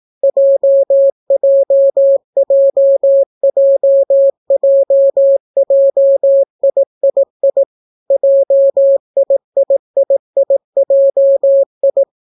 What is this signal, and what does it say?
Morse code, heard as a keyed tone, about 18 wpm, 550 Hz. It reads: JJJJJJIII JIIIIJI